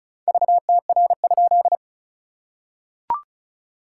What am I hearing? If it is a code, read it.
Morse code, heard as a keyed tone, about 36 wpm, 660 Hz. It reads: VTR?
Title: SOTA QSO Practice 35wpm